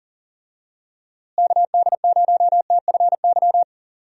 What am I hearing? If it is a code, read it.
Morse code, heard as a keyed tone, about 40 wpm, 700 Hz. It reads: KD0TFY
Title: USA Call Signs - 6 Characters - Repeated 2x - 40wpm